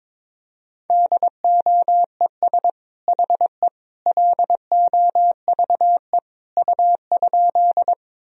Morse code, heard as a keyed tone, about 22 wpm, 700 Hz. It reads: DOES HE LOVE U?
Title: Sentences from Top 400 Words and Abbreviations 22wpm